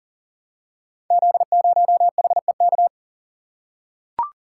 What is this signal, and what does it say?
Morse code, heard as a keyed tone, about 39 wpm, 690 Hz. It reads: Z0HEK
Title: Code Groups - 5 Letter:Numbers - 40wpm